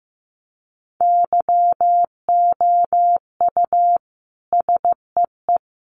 Morse code, heard as a keyed tone, about 15 wpm, 700 Hz. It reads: YOU SEE
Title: Sentences from Top 200 Words 15wpm